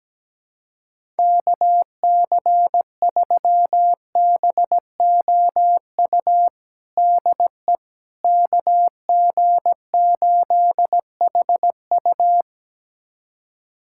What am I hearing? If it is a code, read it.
Morse code, heard as a keyed tone, about 17 wpm, 700 Hz. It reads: KC3BOU DE KG8HU
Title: USA Callsign DE Callsign 17wpm